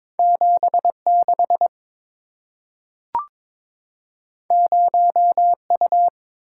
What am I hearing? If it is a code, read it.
Morse code, heard as a keyed tone, about 23 wpm, 700 Hz. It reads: 76 0U
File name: practice / 22wpm / Two Letter-Number Combinations - 22wpm